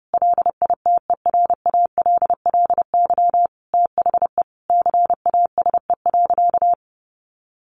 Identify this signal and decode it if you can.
Morse code, heard as a keyed tone, about 30 wpm, 700 Hz. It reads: LITERALLY THE CASE.
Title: Sentences from Top 1400 Words 30wpm